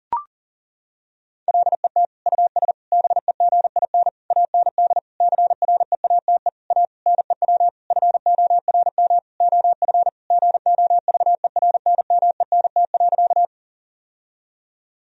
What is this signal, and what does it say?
Morse code, heard as a keyed tone, about 40 wpm, 700 Hz. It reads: LET US BEGIN AND CREATE A NEW FORM OF GOVERNMENT.
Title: Sentences from Top 1000 Words - Review Part 3 - 40wpm